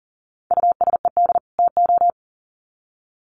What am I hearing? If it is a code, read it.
Morse code, heard as a keyed tone, about 40 wpm, 700 Hz. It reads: USED TO
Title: Sets of 2 Words 40wpm